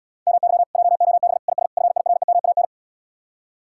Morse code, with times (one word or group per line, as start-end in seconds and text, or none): 0.3-0.6 MY
0.8-1.4 JOB
1.5-1.7 IS
1.8-2.6 FIREMAN